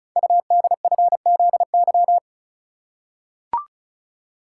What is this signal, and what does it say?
Morse code, heard as a keyed tone, about 33 wpm, 690 Hz. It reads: UDFZY
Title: Code Groups - 5 Letters - 35wpm